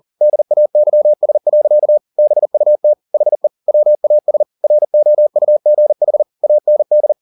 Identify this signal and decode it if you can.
Morse code, heard as a keyed tone, about 40 wpm, 600 Hz. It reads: DAYS. BUT HE WAS ROUGH AND